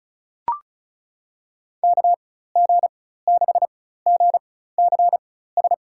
Morse code, heard as a keyed tone, about 34 wpm, spaced as for 16 wpm, 700 Hz. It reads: KG6GCS
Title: USA Call Signs - 4x Character Spacing - 35wpm